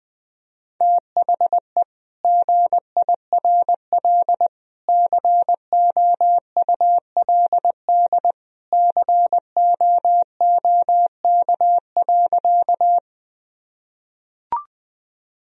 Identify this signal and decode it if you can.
Morse code, heard as a keyed tone, about 20 wpm, 700 Hz. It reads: THE GIRL COULD COOK.
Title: Sentences from Top 1000 Words - Review Part 4 - 20wpm